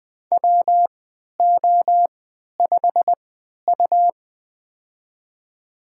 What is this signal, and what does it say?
Morse code, heard as a keyed tone, about 20 wpm, spaced as for 11 wpm, 700 Hz. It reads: WO5U
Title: USA Call Signs - 4 Characters - 3x Character Spacing - 20wpm